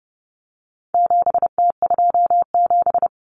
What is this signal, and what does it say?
Morse code, heard as a keyed tone, about 30 wpm, 700 Hz. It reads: 7T27
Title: Common Cut Numbers - 4-Digit 30wpm